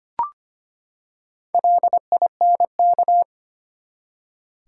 Morse code, heard as a keyed tone, about 25 wpm, 700 Hz. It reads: LINK
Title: Four Letter Words - 25wpm